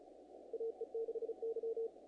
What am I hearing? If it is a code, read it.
Morse code, heard as a keyed tone, about 36 wpm, 450 Hz. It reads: AE6Y
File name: contests / naqp-cw-k5tr-op-k5pi- -1800